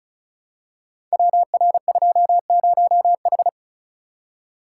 Morse code, heard as a keyed tone, about 36 wpm, 700 Hz. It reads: WR20H
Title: Code Groups - 5 Letter:Numbers - 35wpm